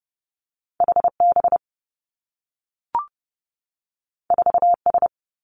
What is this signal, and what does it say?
Morse code, heard as a keyed tone, about 30 wpm, 700 Hz. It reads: H B 4 S
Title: Two Challenging Letter-Number Combinations 30wpm